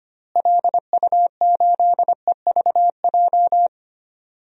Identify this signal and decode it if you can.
Morse code, heard as a keyed tone, about 25 wpm, 700 Hz. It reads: LU8EVJ
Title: DXCC-100 Call Signs - 6 Characters - Repeated 2x - 25wpm